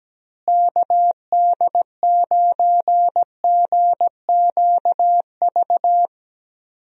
Morse code, heard as a keyed tone, about 17 wpm, 700 Hz. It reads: KD9GQV
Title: USA Call Signs - 6 Characters - Repeated 3x - 17wpm